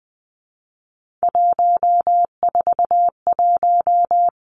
Morse code, heard as a keyed tone, about 20 wpm, 700 Hz. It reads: 141
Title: Three Challenging Letter-Number Combinations 20wpm